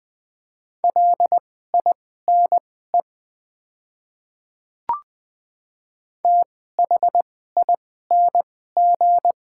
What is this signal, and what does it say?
Morse code, heard as a keyed tone, about 20 wpm, spaced as for 15 wpm, 700 Hz. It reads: LINE THING
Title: 2x Character Spacing - Top 200 Words - 20wpm